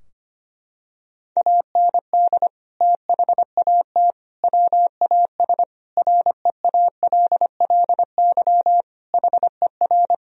Morse code, heard as a keyed tone, about 25 wpm, 700 Hz. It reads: AND THAT WAS REALLY HER
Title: Sentences from Top 1000 Words - Review Part 3 - 25wpm